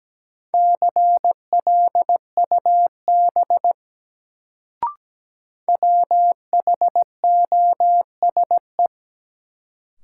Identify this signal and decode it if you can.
Morse code, heard as a keyed tone, about 17 wpm, 700 Hz. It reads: CLUB WHOSE
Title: Sentences from Top 1200 Words - New Word Drill Alt - 17wpm